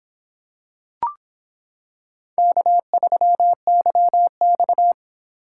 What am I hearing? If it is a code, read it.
Morse code, heard as a keyed tone, about 26 wpm, 700 Hz. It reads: K3YX